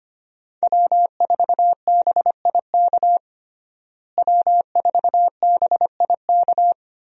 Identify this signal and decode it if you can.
Morse code, heard as a keyed tone, about 25 wpm, 700 Hz. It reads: W4BIK W4BIK
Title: USA Call Signs - 5 Characters - Repeated 2x - 25wpm